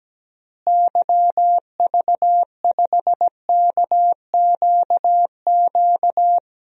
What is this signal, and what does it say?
Morse code, heard as a keyed tone, about 17 wpm, 700 Hz. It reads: YV5KQQ
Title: DXCC-100 Call Signs - 6 Characters - Repeated 3x - 17wpm